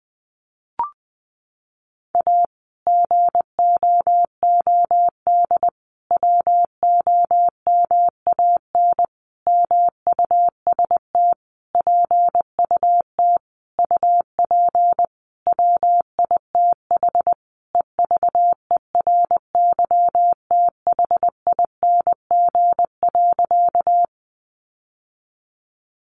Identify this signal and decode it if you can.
Morse code, heard as a keyed tone, about 20 wpm, 700 Hz. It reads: A GOOD WOMAN MUST PUT UP WITH EVERYTHING.
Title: Sentences from Top 1100 Words 20wpm